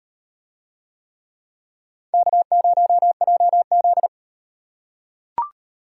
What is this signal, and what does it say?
Morse code, heard as a keyed tone, about 38 wpm, 700 Hz. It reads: K0JZ